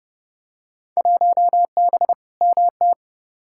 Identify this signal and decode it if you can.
Morse code, heard as a keyed tone, about 30 wpm, 700 Hz. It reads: 1B MT
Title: Field Day Exchanges - 30wpm